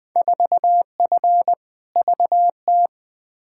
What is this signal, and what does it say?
Morse code, heard as a keyed tone, about 21 wpm, 700 Hz. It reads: 4F VT
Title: Field Day Exchanges Encore - 20wpm